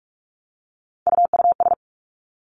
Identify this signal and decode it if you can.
Morse code, heard as a keyed tone, about 46 wpm, 700 Hz. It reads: UUS
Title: Three Challenging Letter-Number Combinations 45wpm